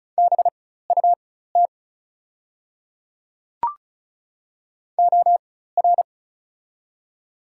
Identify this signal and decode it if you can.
Morse code, heard as a keyed tone, about 36 wpm, spaced as for 16 wpm, 700 Hz. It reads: BUT OR
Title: 4x Character Spacing - Top 100 Words - 35wpm